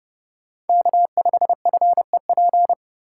Morse code, heard as a keyed tone, about 37 wpm, 700 Hz. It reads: K5FEP
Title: Speed Racing - USA 1x3 Call Signs 30wpm